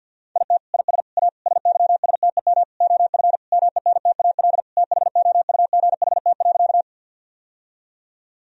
Morse code, heard as a keyed tone, about 50 wpm, 700 Hz. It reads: IT IS A SYSTEM OF MENTAL THOUGHT.
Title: Sentences from Top 1000 Words - Review Part 4 - 50wpm